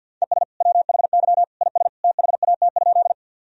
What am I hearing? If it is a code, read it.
Morse code, heard as a keyed tone, about 51 wpm, 700 Hz. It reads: ES WHY IS THAT?
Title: Sentences from Top 200 Words and Abbreviations 50wpm